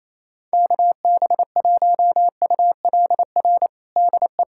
Morse code, heard as a keyed tone, about 28 wpm, 700 Hz. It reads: KB1ULR DE
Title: USA Callsign DE Callsign 28wpm